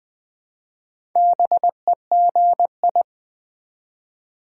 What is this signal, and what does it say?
Morse code, heard as a keyed tone, about 20 wpm, 700 Hz. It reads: BEGI
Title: Build A Word - 1 Letter At A Time - 5-Letter Words - 20wpm